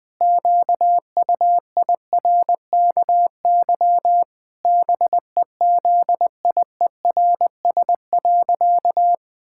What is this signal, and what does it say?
Morse code, heard as a keyed tone, about 20 wpm, 700 Hz. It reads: QUIRKY BEZIERS.
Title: Pangram Challenge v2 - 20wpm